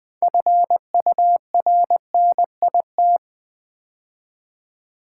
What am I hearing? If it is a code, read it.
Morse code, heard as a keyed tone, about 20 wpm, 700 Hz. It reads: FURNIT
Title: Build A Word - 3 Letters At A Time - 9-Letter Words - 20wpm